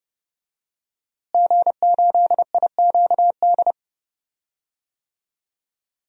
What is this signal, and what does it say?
Morse code, heard as a keyed tone, about 30 wpm, 700 Hz. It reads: G8IQD
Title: DXCC-100 Call Signs - 5 Characters - Repeated 2x - 30wpm